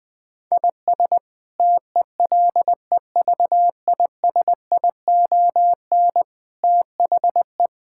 Morse code, heard as a keyed tone, about 20 wpm, 700 Hz. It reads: IS TELEVISION THE
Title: Sentences from Top 1000 Words - Review Part 6 - 20wpm